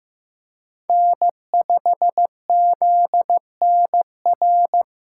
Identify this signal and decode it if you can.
Morse code, heard as a keyed tone, about 15 wpm, 700 Hz. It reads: N5ZNR
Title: USA Call Signs - 5 Characters - 3x Character Spacing - 15wpm